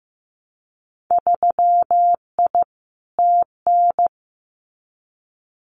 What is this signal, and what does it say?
Morse code, heard as a keyed tone, about 15 wpm, 700 Hz. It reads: 3I TN
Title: Winter Field Day Exchanges 15wpm